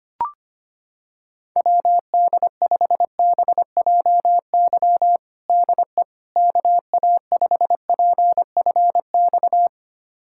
Morse code, heard as a keyed tone, about 25 wpm, 700 Hz. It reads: WD5BJY DE KA5PFX
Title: USA Callsign DE Callsign - 6 Characters - 25wpm